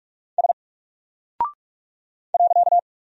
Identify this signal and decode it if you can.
Morse code, heard as a keyed tone, about 44 wpm, 700 Hz. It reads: S .
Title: Single Character (Letter:Number:Punctuation:Prosign) Alt 45wpm